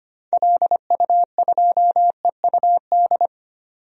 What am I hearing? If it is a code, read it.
Morse code, heard as a keyed tone, about 25 wpm, 700 Hz. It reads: LU2EUD
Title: DXCC-100 Call Signs - 6 Characters - 25wpm